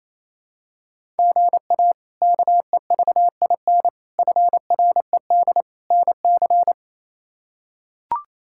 Morse code, heard as a keyed tone, about 28 wpm, 700 Hz. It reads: GA KEVIN FRED NC
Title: K1USN SST QSO Practice - 28wpm